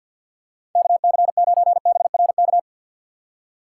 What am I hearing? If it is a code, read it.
Morse code, heard as a keyed tone, about 52 wpm, 700 Hz. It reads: KC9BRK
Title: USA Call Signs - 6 Characters - 3x Character Spacing - 50wpm